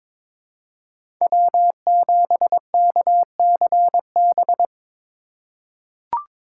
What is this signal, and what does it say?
Morse code, heard as a keyed tone, about 21 wpm, 700 Hz. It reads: W7KCB
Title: USA Call Signs - 5 Characters - 3x Character Spacing - 22wpm